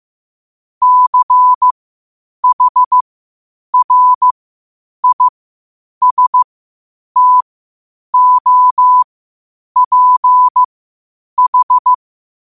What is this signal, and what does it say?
Morse code, heard as a keyed tone, about 15 wpm, spaced as for 8 wpm, 1000 Hz. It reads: CHRISTOPH
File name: qotdfiles / 8wpm